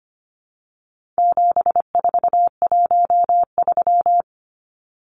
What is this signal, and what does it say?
Morse code, heard as a keyed tone, about 25 wpm, 700 Hz. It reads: 7413
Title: Numbers - 4-Digit 25wpm